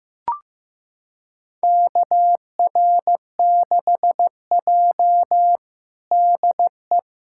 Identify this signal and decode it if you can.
Morse code, heard as a keyed tone, about 15 wpm, 700 Hz. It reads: KR6J DE
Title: USA Callsign DE Callsign - 4 Characters - 15wpm